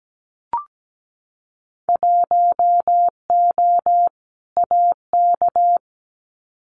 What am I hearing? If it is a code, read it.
Morse code, heard as a keyed tone, about 17 wpm, 700 Hz. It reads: 1O AK
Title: Winter Field Day Exchanges Encore 17wpm